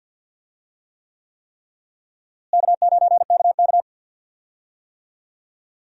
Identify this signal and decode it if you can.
Morse code, heard as a keyed tone, about 52 wpm, 700 Hz. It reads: K9KK
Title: USA Call Signs - 4 Characters - Repeated 3x - 50wpm